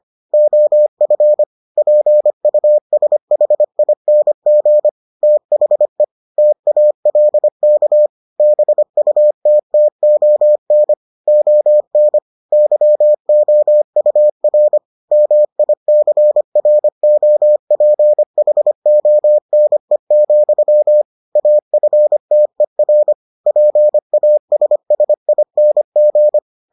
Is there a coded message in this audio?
Yes: Morse code from 0.3 to 26.4 s, heard as a keyed tone, about 25 wpm, 600 Hz. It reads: OF PUSHING THE TALK BUTTON ON YOUR MICROPHONE, AFTER PASSING